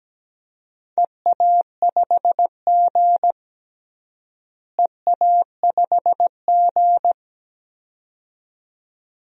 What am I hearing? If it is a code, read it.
Morse code, heard as a keyed tone, about 17 wpm, 700 Hz. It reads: EA5G EA5G
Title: DXCC-100 Call Signs - 4 Characters - Repeated 2x - 17wpm